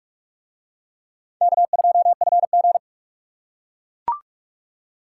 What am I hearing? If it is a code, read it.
Morse code, heard as a keyed tone, about 38 wpm, 700 Hz. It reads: K2FG